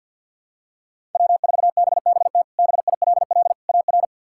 Morse code, heard as a keyed tone, about 50 wpm, 700 Hz. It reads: W4BBT BILL AR
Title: K1USN SST QSO Practice - 50wpm